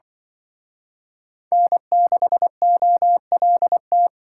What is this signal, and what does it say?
Morse code, heard as a keyed tone, about 25 wpm, 700 Hz. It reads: N6OLT
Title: Speed Racing - USA 1x3 Call Signs 20wpm